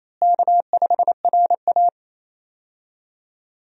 Morse code, heard as a keyed tone, about 28 wpm, 700 Hz. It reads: K5RA
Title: USA Call Signs - 4 Characters - Repeated 2x - 28wpm